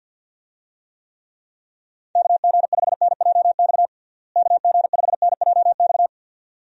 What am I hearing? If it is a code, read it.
Morse code, heard as a keyed tone, about 51 wpm, 700 Hz. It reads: KG5NJX KG5NJX
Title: USA Call Signs - 6 Characters - Repeated 2x - 50wpm